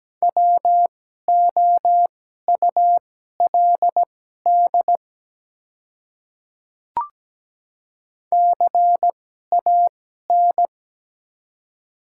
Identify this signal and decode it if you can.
Morse code, heard as a keyed tone, about 17 wpm, spaced as for 12 wpm, 700 Hz. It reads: WOULD CAN
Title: 2x Character Spacing - Top 100 Words - 17wpm